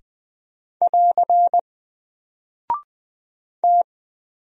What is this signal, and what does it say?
Morse code, heard as a keyed tone, about 20 wpm, 700 Hz. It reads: <AR> T
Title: Single Character (Letter:Number:Punctuation:Prosign) 20wpm